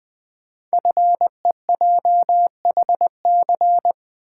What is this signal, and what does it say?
Morse code, heard as a keyed tone, about 20 wpm, 700 Hz. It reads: FEJHC
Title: Code Groups - 5 Letters - 20wpm